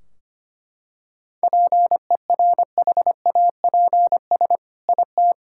Morse code, heard as a keyed tone, about 25 wpm, 700 Hz. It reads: PERHAPS IT
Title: Sentences from Top 1600 Words 25wpm